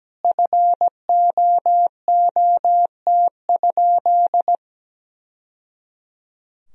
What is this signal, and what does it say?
Morse code, heard as a keyed tone, about 17 wpm, 700 Hz. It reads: FOOT?
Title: Sentences from Top 1000 Words - Review Part 3 - 17wpm